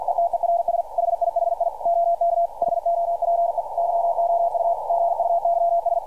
Morse code, heard as a keyed tone, about 52 wpm, 680 Hz, with radio noise. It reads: ETERS HIHI OK HOPE YOU GUYS
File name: archived / CRSnet